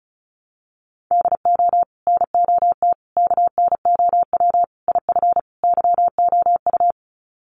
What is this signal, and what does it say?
Morse code, heard as a keyed tone, about 34 wpm, 700 Hz. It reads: DO NOT KNOW IF YOU
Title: Sets of 5 Words 35wpm